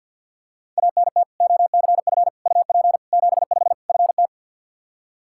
Morse code, heard as a keyed tone, about 48 wpm, 700 Hz. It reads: ANT OCF UP 75 FT
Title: Practice QSO - Exchange 2 - Info - 50wpm